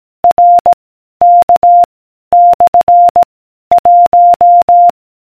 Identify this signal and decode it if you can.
Morse code, heard as a keyed tone, about 18 wpm, 700 Hz. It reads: R K / 1